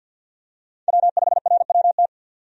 Morse code, heard as a keyed tone, about 50 wpm, 700 Hz. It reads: W5RWT
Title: USA Call Signs - 5 Characters - 50wpm